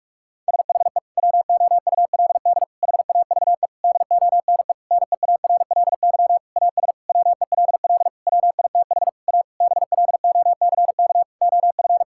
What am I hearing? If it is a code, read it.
Morse code, heard as a keyed tone, about 46 wpm, 700 Hz. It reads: SHE WOULD HAVE DONE NEARLY AS WELL WITH A BLOCK OF
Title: Sentences from Top 1600 Words 45wpm